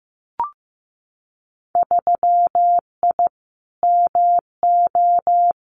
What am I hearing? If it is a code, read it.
Morse code, heard as a keyed tone, about 15 wpm, 700 Hz. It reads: 3I MO
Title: Winter Field Day Exchanges 15wpm